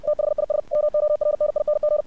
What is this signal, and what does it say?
Morse code, heard as a keyed tone, about 58 wpm, 600 Hz. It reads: THIS MORNING
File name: archived / CRSnet